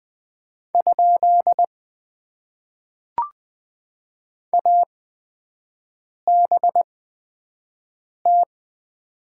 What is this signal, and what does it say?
Morse code, heard as a keyed tone, about 20 wpm, spaced as for 5 wpm, 700 Hz. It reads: ? ABT
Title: Top 90 QSO Elements - 8x Character Spacing - 20wpm